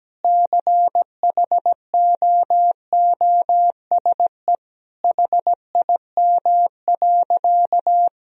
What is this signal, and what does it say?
Morse code, heard as a keyed tone, about 17 wpm, 700 Hz. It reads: CHOOSE HIM.
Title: Sentences from Top 1000 Words - Review Part 4 - 17wpm